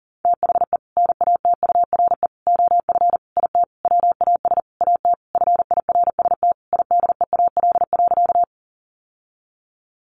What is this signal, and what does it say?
Morse code, heard as a keyed tone, about 40 wpm, 700 Hz. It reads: THE NATURE OF IT WAS AT FIRST IDEAL.
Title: Sentences from Top 1400 Words 40wpm